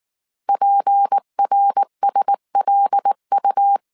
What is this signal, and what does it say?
Morse code, heard as a keyed tone, about 19 wpm, 800 Hz. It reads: PRSLU
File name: CW / DET / 1 6 1500C S A 18wpm 2255